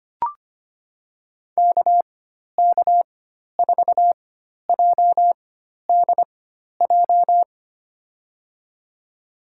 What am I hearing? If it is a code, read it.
Morse code, heard as a keyed tone, about 25 wpm, spaced as for 12 wpm, 700 Hz. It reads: KK4JDJ
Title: USA Call Signs - 4x Character Spacing - 25wpm